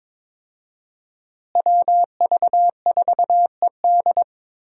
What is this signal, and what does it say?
Morse code, heard as a keyed tone, about 21 wpm, 700 Hz. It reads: WV4ED
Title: USA Call Signs - 5 Characters - Repeated 2x - 22wpm